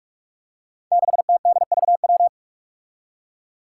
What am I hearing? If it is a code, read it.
Morse code, heard as a keyed tone, about 42 wpm, 700 Hz. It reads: 6TDVW
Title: Five Challenging Letter-Number Combinations 45wpm